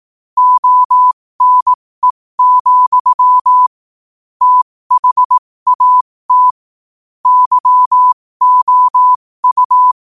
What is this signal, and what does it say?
Morse code, heard as a keyed tone, about 18 wpm, spaced as for 15 wpm, 1000 Hz. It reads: ONE, THAT YOU
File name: qotdfiles / 15wpm